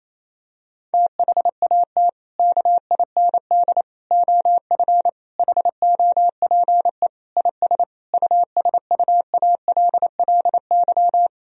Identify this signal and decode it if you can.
Morse code, heard as a keyed tone, about 28 wpm, 700 Hz. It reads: THAT KIND OF HOPE IS USUALLY
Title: Sentences from Top 1000 Words - Review Part 3 - 28wpm